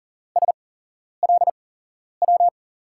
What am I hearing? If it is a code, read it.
Morse code, heard as a keyed tone, about 42 wpm, spaced as for 11 wpm, 700 Hz. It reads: SLW
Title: USA Call Signs - 6 Characters - 8x Character Spacing - 40wpm